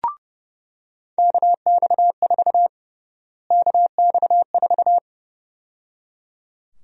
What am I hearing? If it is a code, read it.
Morse code, heard as a keyed tone, about 32 wpm, 700 Hz. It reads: KX4 KX4
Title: USA Call Sign Prefixes - Repeated 2x - 30wpm